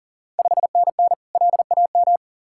Morse code, heard as a keyed tone, about 40 wpm, 700 Hz. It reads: 5NN LAM
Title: Ontario QSO Party Exchange 40wpm